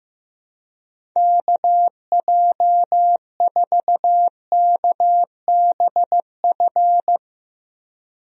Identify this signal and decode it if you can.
Morse code, heard as a keyed tone, about 15 wpm, 700 Hz. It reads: KJ4KBF
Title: USA Call Signs - 6 Characters - 15wpm